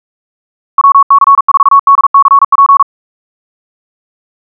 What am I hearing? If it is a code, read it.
Morse code, heard as a keyed tone, about 42 wpm, 1110 Hz. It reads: WC4DCP
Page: SampleHST1.wav <-- 158cpm (=47wpm(paris)) 1111Hz sinusoidal waveform